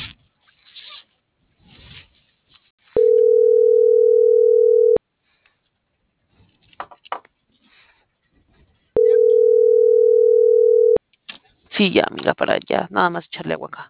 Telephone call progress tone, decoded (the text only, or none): ringback tone